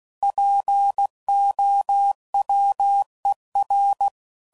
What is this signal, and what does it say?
Morse code, heard as a keyed tone, about 16 wpm, 800 Hz. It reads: POWER
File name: CW Code Groups @15WPM / Technical Subjects / PwrSupply